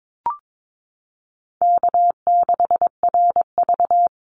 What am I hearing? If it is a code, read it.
Morse code, heard as a keyed tone, about 22 wpm, 700 Hz. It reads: K6RV
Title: Easy Call Signs Repeated 2x - 22wpm